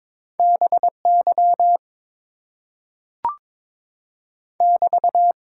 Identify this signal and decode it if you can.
Morse code, heard as a keyed tone, about 22 wpm, 700 Hz. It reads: BY <BT>
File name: Top 90 QSO Elements - 8x Character Spacing - 22wpm